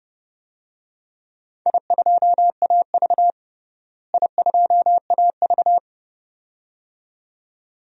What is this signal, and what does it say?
Morse code, heard as a keyed tone, about 30 wpm, 700 Hz. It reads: I2AV I2AV
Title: DXCC-100 Call Signs - 4 Characters - Repeated 2x - 30wpm